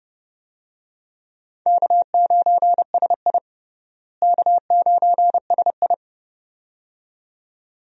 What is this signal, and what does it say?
Morse code, heard as a keyed tone, about 30 wpm, 700 Hz. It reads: K9SI K9SI
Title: USA Call Signs - 4 Characters - Repeated 2x - 30wpm